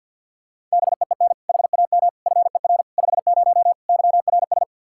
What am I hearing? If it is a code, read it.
Morse code, heard as a keyed tone, about 50 wpm, 700 Hz. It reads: BEEN HAM FER 50 YRS
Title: K1USN SST QSO Practice 50wpm